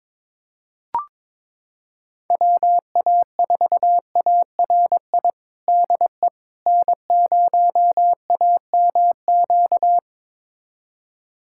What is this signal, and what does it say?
Morse code, heard as a keyed tone, about 22 wpm, 700 Hz. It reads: WA4ARI DE N0AMQ
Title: USA Callsign DE Callsign 22wpm